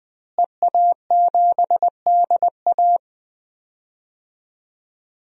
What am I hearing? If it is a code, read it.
Morse code, heard as a keyed tone, about 21 wpm, 700 Hz. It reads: EA7DA
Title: DXCC-100 Call Signs - 5 Characters - Repeated 3x - 20wpm